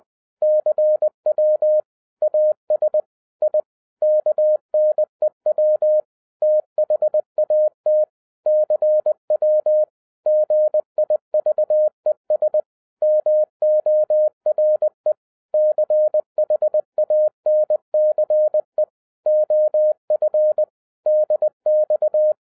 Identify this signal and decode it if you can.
Morse code, heard as a keyed tone, about 20 wpm, 610 Hz. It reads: CW AS I KNEW THAT CW GIVES MORE CHANCE OF DX
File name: 2012-01-mp3 / n05-CAME BACK TO HAM RADIO AFTER 30 YEARS OF INTERVAL-20wpm